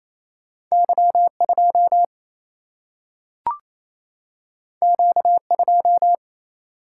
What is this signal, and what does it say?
Morse code, heard as a keyed tone, about 28 wpm, 700 Hz. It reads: Y2 Q2
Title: Single-Letter Single-Number 28wpm